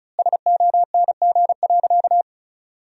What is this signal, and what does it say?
Morse code, heard as a keyed tone, about 36 wpm, 700 Hz. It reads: SONG.
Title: 4x Word-Spacing - Sentences from Top 400 Words - 35wpm